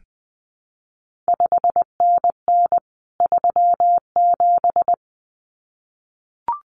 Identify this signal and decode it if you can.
Morse code, heard as a keyed tone, about 20 wpm, 700 Hz. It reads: 5NN 37
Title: CQ WW DX Contest Exchange Encore - 20wpm